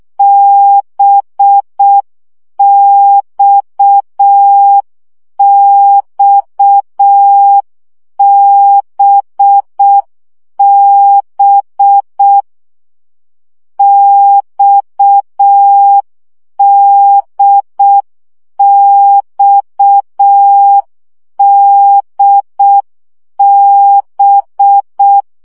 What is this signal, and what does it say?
Morse code, heard as a keyed tone, about 6 wpm, 800 Hz. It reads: BXXBB XDXDB
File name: CW gyakorlas d b x 5 baud